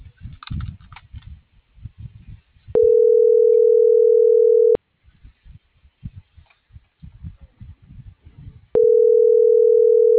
Ringback tone (North America), 440 + 480 Hz.